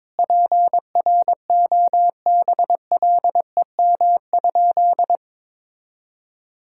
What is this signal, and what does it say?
Morse code, heard as a keyed tone, about 22 wpm, 700 Hz. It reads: PROBLEM?
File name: practice / 22wpm / Sentences with 8-Word Phrases - 22wpm